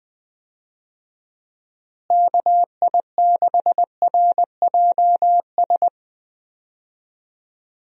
Morse code, heard as a keyed tone, about 20 wpm, 700 Hz. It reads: KI6RJS
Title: USA Call Signs - 6 Characters - Repeated 2x - 20wpm